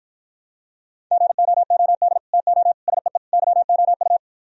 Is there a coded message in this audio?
Yes: Morse code from 1.1 to 4.2 s, heard as a keyed tone, about 44 wpm, 700 Hz. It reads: GOOD TO SEE YOU